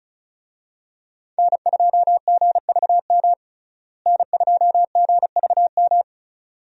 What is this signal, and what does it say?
Morse code, heard as a keyed tone, about 34 wpm, 700 Hz. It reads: N2GVM N2GVM
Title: USA Call Signs - 5 Characters - Repeated 2x - 35wpm